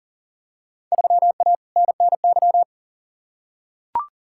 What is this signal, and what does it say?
Morse code, heard as a keyed tone, about 40 wpm, 700 Hz. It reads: 3A NNY
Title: Field Day Exchanges - 40wpm